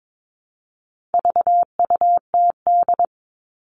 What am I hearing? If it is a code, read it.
Morse code, heard as a keyed tone, about 22 wpm, 700 Hz. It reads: VUTD
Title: All Cut Numbers - 4-Digit 22wpm